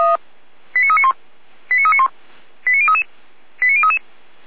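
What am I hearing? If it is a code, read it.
Five-tone selective call (ZVEI), heard as quick runs of five tones, tones of about 70 ms. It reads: 89391 89391 89033 89033
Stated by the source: ZVEI five-tone selective call received via SDR.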